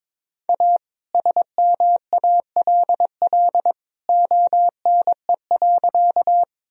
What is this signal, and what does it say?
Morse code, heard as a keyed tone, about 22 wpm, 700 Hz. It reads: A SMALL ONE.
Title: Sentences from Top 1000 Words - Review Part 1 - 22wpm